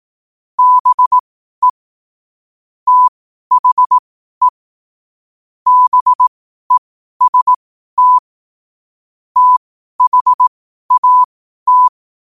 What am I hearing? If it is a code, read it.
Morse code, heard as a keyed tone, about 18 wpm, spaced as for 12 wpm, 1000 Hz. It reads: BE THE BEST THAT
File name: 2026-01-07-12wpm